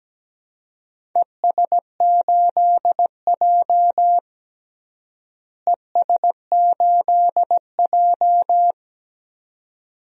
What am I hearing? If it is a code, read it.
Morse code, heard as a keyed tone, about 17 wpm, 700 Hz. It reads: ES8J ES8J
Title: DXCC-100 Call Signs - 4 Characters - Repeated 2x - 17wpm